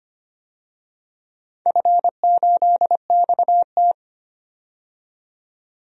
Morse code, heard as a keyed tone, about 25 wpm, 700 Hz. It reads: F8XT
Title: DXCC-100 Call Signs - 4 Characters - 25wpm